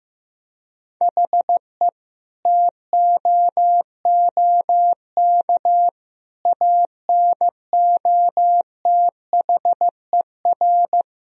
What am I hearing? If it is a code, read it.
Morse code, heard as a keyed tone, about 16 wpm, 700 Hz. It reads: HE TOOK ANOTHER
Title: Sentences from Top 1000 Words - Review Part 2 - 15wpm